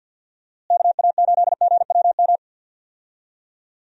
Morse code, heard as a keyed tone, about 52 wpm, 700 Hz. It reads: KA8GWM
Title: USA Call Signs - 6 Characters - Repeated 2x - 50wpm